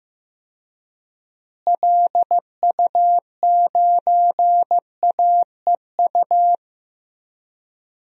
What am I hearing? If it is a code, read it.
Morse code, heard as a keyed tone, about 15 wpm, 700 Hz. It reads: LU9AEU
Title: DXCC-100 Call Signs - 6 Characters - Repeated 2x - 15wpm